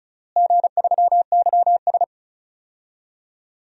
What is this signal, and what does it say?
Morse code, heard as a keyed tone, about 36 wpm, 700 Hz. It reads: G3YS
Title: DXCC-100 Call Signs - 4 Characters - Repeated 2x - 35wpm